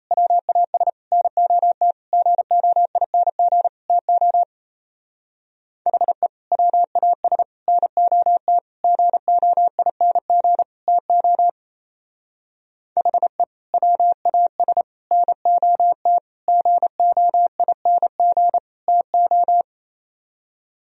Morse code, heard as a keyed tone, about 37 wpm, 700 Hz. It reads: WAS NOT GOING TO HE WAS NOT GOING TO HE WAS NOT GOING TO